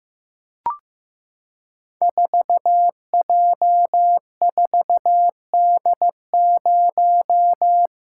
Morse code, heard as a keyed tone, about 15 wpm, 700 Hz. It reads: 4J4D0
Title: Code Groups - 5 Letter:Numbers - 15wpm